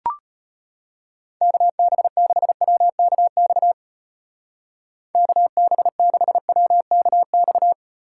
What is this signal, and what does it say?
Morse code, heard as a keyed tone, about 32 wpm, 700 Hz. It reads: KB6WKX KB6WKX